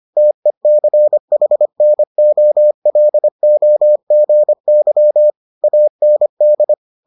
Morse code, heard as a keyed tone, about 25 wpm, 600 Hz. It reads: TECHNOLOGY AND